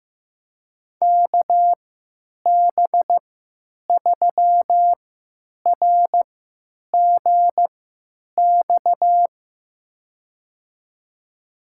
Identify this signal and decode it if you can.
Morse code, heard as a keyed tone, about 15 wpm, spaced as for 9 wpm, 700 Hz. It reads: KB3RGX
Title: USA Call Signs - 6 Characters - 3x Character Spacing - 15wpm